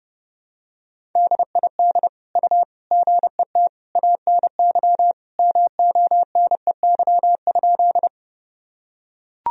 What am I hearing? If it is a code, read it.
Morse code, heard as a keyed tone, about 30 wpm, 700 Hz. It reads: DID U GET ANY MONEY?
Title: Sentences from Top 400 Words and Abbreviations 30wpm